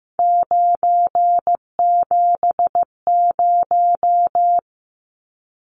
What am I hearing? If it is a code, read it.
Morse code, heard as a keyed tone, about 15 wpm, 700 Hz. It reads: 970
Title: Numbers - 3-Digit 15wpm